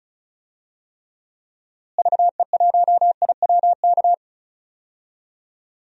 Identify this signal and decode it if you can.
Morse code, heard as a keyed tone, about 36 wpm, 700 Hz. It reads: VE1IWK
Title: DXCC-100 Call Signs - 6 Characters - Repeated 2x - 35wpm